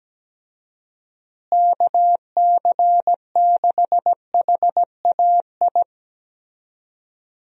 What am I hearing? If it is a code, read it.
Morse code, heard as a keyed tone, about 17 wpm, 700 Hz. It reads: KC6HAI
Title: USA Call Signs - 6 Characters - Repeated 3x - 17wpm